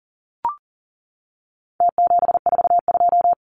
Morse code, heard as a keyed tone, about 40 wpm, 700 Hz. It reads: T742
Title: Numbers - Contest Style - 4-Digit 40wpm